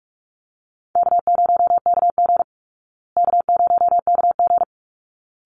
Morse code, heard as a keyed tone, about 44 wpm, 700 Hz. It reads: K0KG K0KG
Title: Easy Call Signs Repeated 2x - 45wpm